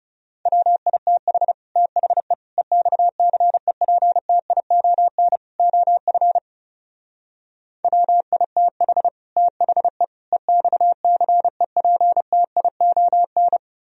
Morse code, heard as a keyed tone, about 37 wpm, 700 Hz. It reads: WITH THE EXCEPTION OF WITH THE EXCEPTION
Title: Speed-Racing - Sets of 4 Words 30wpm